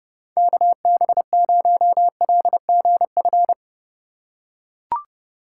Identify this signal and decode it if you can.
Morse code, heard as a keyed tone, about 30 wpm, 700 Hz. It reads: KB0LGF
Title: K1USN SST QSO Practice - 30wpm